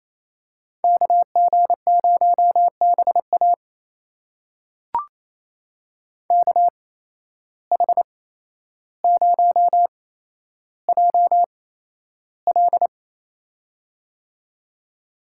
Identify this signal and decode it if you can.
Morse code, heard as a keyed tone, about 27 wpm, spaced as for 12 wpm, 700 Hz. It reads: KG0BA KH0JL
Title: USA Call Signs - 5 Characters - 8x Character Spacing - 28wpm